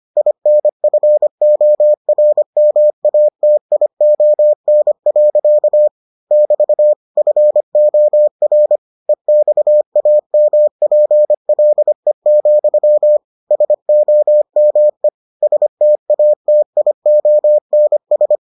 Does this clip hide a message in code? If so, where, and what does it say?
Yes: Morse code from 0.2 to 18.4 s, heard as a keyed tone, about 25 wpm, 600 Hz. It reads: INFORMATION. <BT> FOR EXAMPLE, SOME STATIONS